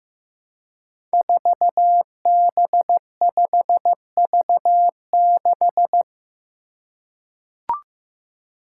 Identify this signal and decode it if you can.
Morse code, heard as a keyed tone, about 15 wpm, 700 Hz. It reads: 4B5V6
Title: Five Challenging Letter-Number Combinations 15wpm